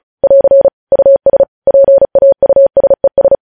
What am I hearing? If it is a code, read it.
Morse code, heard as a keyed tone, about 36 wpm, 550 Hz. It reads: <AR> US PAUSES
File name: podcast / News headlines-1x-35x35